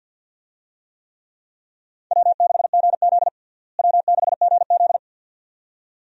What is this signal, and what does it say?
Morse code, heard as a keyed tone, about 52 wpm, 700 Hz. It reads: W6GZ W6GZ
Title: USA Call Signs - 4 Characters - Repeated 2x - 50wpm